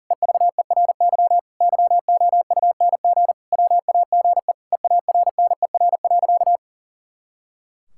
Morse code, heard as a keyed tone, about 40 wpm, 700 Hz. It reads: EVERY YOUNG WAGE EARNER.
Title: Pangram Challenge v2 - 40wpm